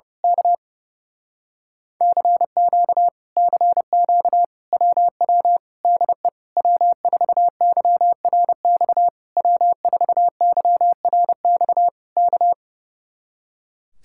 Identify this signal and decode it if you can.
Morse code, heard as a keyed tone, about 28 wpm, 700 Hz. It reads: K CQ CQ WW DE W4YRX W4YRX K
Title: Speed-Racing Common QSO Phrases Encore - 30wpm